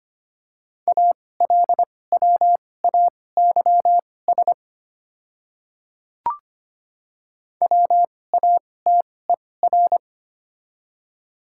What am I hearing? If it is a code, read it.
Morse code, heard as a keyed tone, about 25 wpm, spaced as for 18 wpm, 690 Hz. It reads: ALWAYS WATER
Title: 2x Character Spacing - Top 300 Words - 25wpm